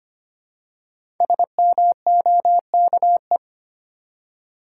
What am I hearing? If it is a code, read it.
Morse code, heard as a keyed tone, about 25 wpm, 700 Hz. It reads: SMOKE
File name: 5 Letter Words - First 3 Letters Given - 25wpm